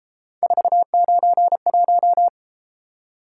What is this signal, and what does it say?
Morse code, heard as a keyed tone, about 31 wpm, 700 Hz. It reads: 491